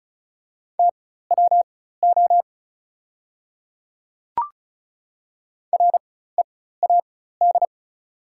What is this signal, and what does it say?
Morse code, heard as a keyed tone, about 36 wpm, spaced as for 16 wpm, 700 Hz. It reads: TWO READ
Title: 4x Character Spacing - Top 200 Words - 35wpm